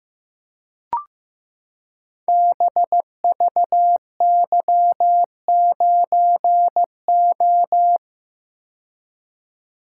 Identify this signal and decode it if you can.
Morse code, heard as a keyed tone, about 15 wpm, 700 Hz. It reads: BVY9O
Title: Code Groups - 5 Letter:Numbers - 15wpm